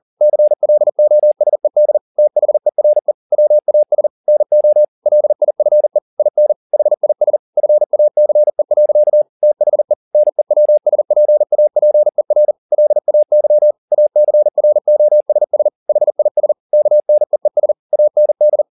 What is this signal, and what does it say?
Morse code, heard as a keyed tone, about 40 wpm, 600 Hz. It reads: CLOSED THERE WAS NO LIFE IN HIS FACE. THE NEWSPAPER LAY ACROSS HIS KNEES AND